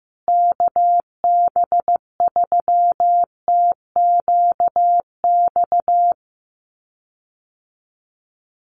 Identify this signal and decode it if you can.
Morse code, heard as a keyed tone, about 15 wpm, 700 Hz. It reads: KB3TQX
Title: USA Call Signs Repeated 2x 15wpm